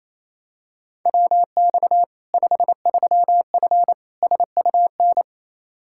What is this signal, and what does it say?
Morse code, heard as a keyed tone, about 28 wpm, 700 Hz. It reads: WX 53F SUN
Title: Practice QSO - Exchange 2 - Info - 28wpm